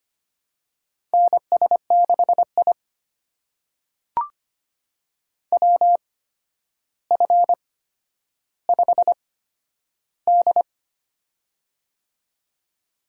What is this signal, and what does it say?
Morse code, heard as a keyed tone, about 25 wpm, spaced as for 11 wpm, 700 Hz. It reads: NS6I WF5D
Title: USA Call Signs - 4 Characters - 8x Character Spacing - 25wpm